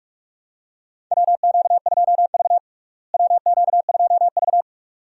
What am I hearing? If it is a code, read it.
Morse code, heard as a keyed tone, about 41 wpm, 700 Hz. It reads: WQ2V WQ2V
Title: USA Call Signs - 4 Characters - Repeated 2x - 45wpm